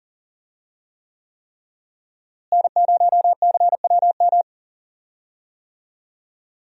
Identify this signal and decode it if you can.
Morse code, heard as a keyed tone, about 40 wpm, 700 Hz. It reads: N0CWM
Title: USA Call Signs - 5 Characters - Repeated 3x - 40wpm